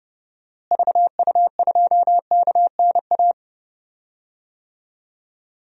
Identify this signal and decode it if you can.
Morse code, heard as a keyed tone, about 30 wpm, 700 Hz. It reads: VU2KNA
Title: DXCC-100 Call Signs - 6 Characters - Repeated 2x - 30wpm